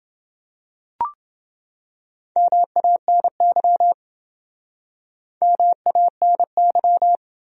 Morse code, heard as a keyed tone, about 27 wpm, 700 Hz. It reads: MANY MANY